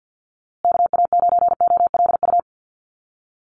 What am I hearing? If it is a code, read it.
Morse code, heard as a keyed tone, about 50 wpm, 700 Hz. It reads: KA9OLU
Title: USA Call Signs Repeated 2x 50wpm